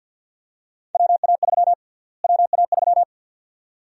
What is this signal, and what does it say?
Morse code, heard as a keyed tone, about 50 wpm, 700 Hz. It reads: WA3 WA3
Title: USA Call Sign Prefixes - Repeated 2x - 50wpm